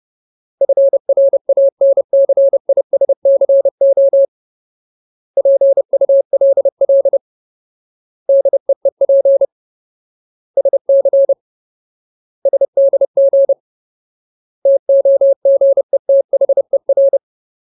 Morse code, heard as a keyed tone, about 30 wpm, 550 Hz. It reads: FRANCISCO PULL DEEP SC SDG TOGETHER
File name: Ultimate Word List / 30WPM Ultimate